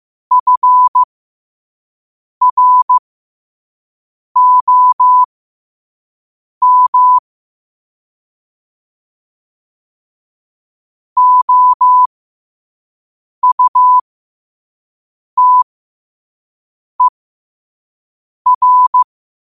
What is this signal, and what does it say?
Morse code, heard as a keyed tone, about 15 wpm, spaced as for 5 wpm, 1000 Hz. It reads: FROM OUTER
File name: qotdfiles / 5wpm